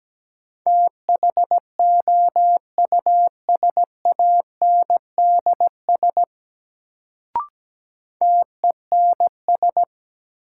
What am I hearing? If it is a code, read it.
Morse code, heard as a keyed tone, about 17 wpm, 700 Hz. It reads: THOUSANDS TENS
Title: Sets of 4 Words - New Word Drill - 17wpm